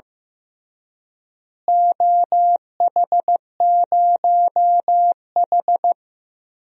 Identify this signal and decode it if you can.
Morse code, heard as a keyed tone, about 15 wpm, 700 Hz. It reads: OH0H
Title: DXCC-100 Call Signs - 4 Characters - Repeated 2x - 15wpm